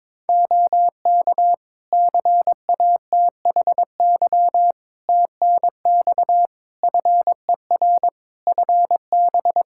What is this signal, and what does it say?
Morse code, heard as a keyed tone, about 22 wpm, 700 Hz. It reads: OK CATHY TNX FER FB
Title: Practice QSO - Exchange 4 - Ending - 22wpm